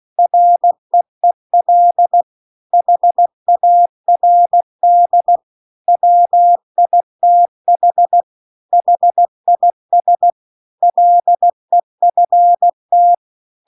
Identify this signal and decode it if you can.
Morse code, heard as a keyed tone, about 16 wpm, 700 Hz. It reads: REEL HARD WITH HIS LEFT